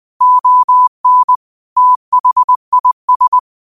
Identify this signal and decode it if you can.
Morse code, heard as a keyed tone, about 20 wpm, 1000 Hz. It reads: ON THIS
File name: 20wpm